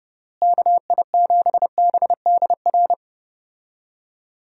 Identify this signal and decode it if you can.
Morse code, heard as a keyed tone, about 30 wpm, 700 Hz. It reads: KI7BDR
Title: USA Call Signs - 6 Characters - Repeated 3x - 30wpm